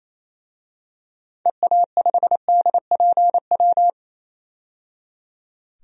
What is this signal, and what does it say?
Morse code, heard as a keyed tone, about 28 wpm, 700 Hz. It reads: EA5DPW
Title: DXCC-100 Call Signs - 6 Characters - 28wpm